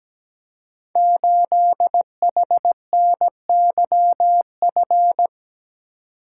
Morse code, heard as a keyed tone, about 17 wpm, 700 Hz. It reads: 8HNYF
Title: Code Groups - 5 Letter:Numbers - 17wpm